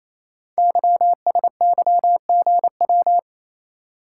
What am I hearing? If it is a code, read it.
Morse code, heard as a keyed tone, about 27 wpm, 700 Hz. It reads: YSYGW
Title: Code Groups - 5 Letters - 28wpm